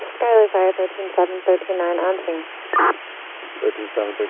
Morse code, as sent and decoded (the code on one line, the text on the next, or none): none